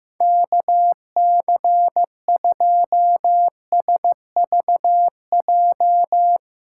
Morse code, heard as a keyed tone, about 15 wpm, 700 Hz. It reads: KC2SVJ
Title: USA Callsign DE Callsign - 6 Characters - 15wpm